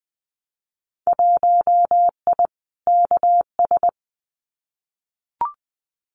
Morse code, heard as a keyed tone, about 20 wpm, 700 Hz. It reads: 1I KS
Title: Winter Field Day Exchanges Encore 20wpm